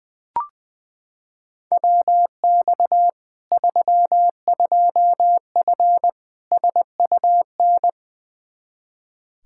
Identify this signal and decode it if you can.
Morse code, heard as a keyed tone, about 21 wpm, 700 Hz. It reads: WX 32F SUN
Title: Practice QSO - Exchange 2 - Info - 20wpm